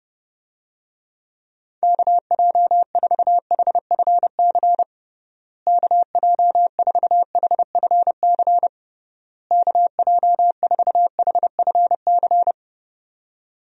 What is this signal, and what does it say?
Morse code, heard as a keyed tone, about 32 wpm, 700 Hz. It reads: KJ4HFC KJ4HFC KJ4HFC
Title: USA Call Signs - 6 Characters - Repeated 3x - 30wpm